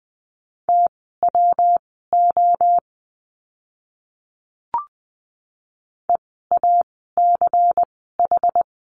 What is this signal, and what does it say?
Morse code, heard as a keyed tone, about 20 wpm, spaced as for 14 wpm, 700 Hz. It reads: TWO EACH
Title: 192 Lesson 31f - Letter v - Words 20wpm